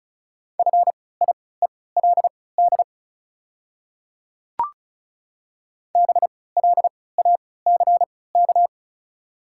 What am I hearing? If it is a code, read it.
Morse code, heard as a keyed tone, about 36 wpm, spaced as for 20 wpm, 700 Hz. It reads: FIELD BLACK
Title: 3x Character Spacing - Top 500 Words - 35wpm